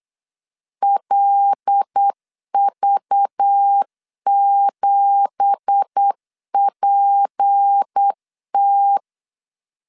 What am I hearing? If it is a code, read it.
Morse code, heard as a keyed tone, about 8 wpm, 800 Hz. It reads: LV7PT
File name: CW / DET / 1 9 1500C S B 8wpm 5624